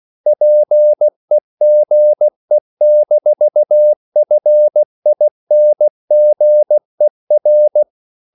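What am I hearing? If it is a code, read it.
Morse code, heard as a keyed tone, about 16 wpm, 600 Hz. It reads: PEGE-FINGER
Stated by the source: Slow (QRS) Morse code at 16 wpm, from CW med Gnister.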